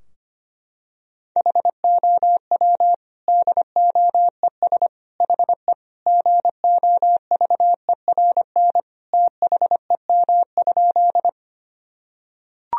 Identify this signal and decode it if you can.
Morse code, heard as a keyed tone, about 25 wpm, 700 Hz. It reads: HOW DOES HE GOVERN THEM?
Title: Sentences from Top 1000 Words - Review Part 2 - 25wpm